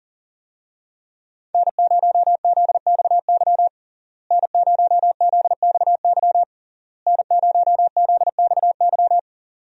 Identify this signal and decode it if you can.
Morse code, heard as a keyed tone, about 40 wpm, 700 Hz. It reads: N0ZXY N0ZXY N0ZXY
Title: USA Call Signs - 5 Characters - Repeated 3x - 40wpm